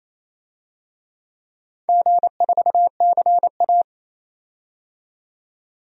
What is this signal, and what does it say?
Morse code, heard as a keyed tone, about 27 wpm, 690 Hz. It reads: G4CA
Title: DXCC-100 Call Signs - 4 Characters - 28wpm